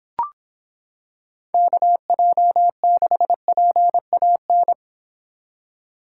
Morse code, heard as a keyed tone, about 26 wpm, 700 Hz. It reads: KJ6PAN